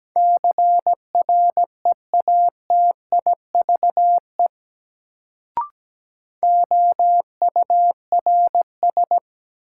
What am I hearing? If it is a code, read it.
Morse code, heard as a keyed tone, about 17 wpm, 700 Hz. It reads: CREATIVE OURS
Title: Sentences from Top 1000 Words - Review Part 5 - New Word Drill - 17wpm